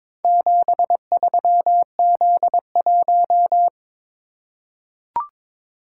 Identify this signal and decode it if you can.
Morse code, heard as a keyed tone, about 22 wpm, 700 Hz. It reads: 73Z1
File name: practice / 22wpm / Four Letter-Number Combinations - 22wpm